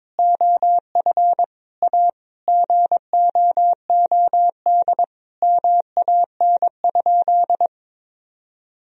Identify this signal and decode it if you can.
Morse code, heard as a keyed tone, about 22 wpm, 700 Hz. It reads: OF A GOOD MAN?
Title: Sentences from Top 1000 Words - Review Part 4 - 22wpm